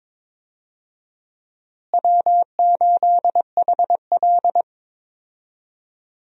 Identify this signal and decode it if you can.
Morse code, heard as a keyed tone, about 21 wpm, 700 Hz. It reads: W8HL
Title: USA Call Signs - 4 Characters - Repeated 2x - 22wpm